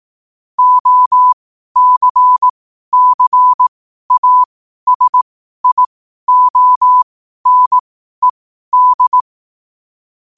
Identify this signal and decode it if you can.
Morse code, heard as a keyed tone, about 18 wpm, spaced as for 12 wpm, 1000 Hz. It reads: OCCASIONED
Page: New quotes every day in morse code at 12 Words per minute.